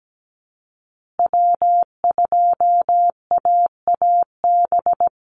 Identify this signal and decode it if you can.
Morse code, heard as a keyed tone, about 17 wpm, 700 Hz. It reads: W2AAB
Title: Call Signs Repeated 2x - 17wpm